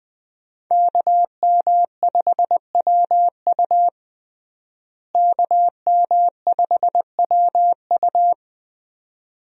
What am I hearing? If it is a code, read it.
Morse code, heard as a keyed tone, about 20 wpm, 700 Hz. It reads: KM5WU KM5WU
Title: USA Call Signs - 5 Characters - Repeated 2x - 20wpm